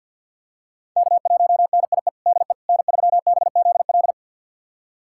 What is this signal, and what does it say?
Morse code, heard as a keyed tone, about 52 wpm, 700 Hz. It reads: K1NIE DE N3BZL
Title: USA Callsign DE Callsign - 5 Characters - 50wpm